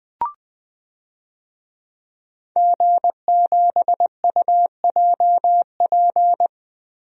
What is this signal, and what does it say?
Morse code, heard as a keyed tone, about 20 wpm, 700 Hz. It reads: G7UJP
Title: DXCC-100 Call Signs - 5 Characters - 20wpm